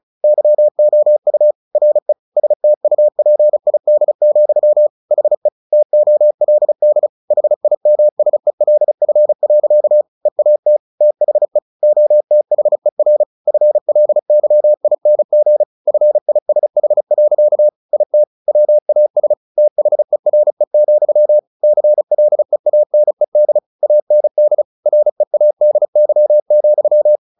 Morse code, heard as a keyed tone, about 37 wpm, 600 Hz. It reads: YOU RE STUPID, HE TOLD HIMSELF. EAT THE OTHER FLYING FISH. IT WAS THERE, CLEANED AND READY,
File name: the old man and the sea 2 35wpm